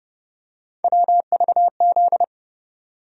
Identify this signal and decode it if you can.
Morse code, heard as a keyed tone, about 30 wpm, 700 Hz. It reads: WVZ
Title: USA Call Sign Postfixes - 3 Letters 30wpm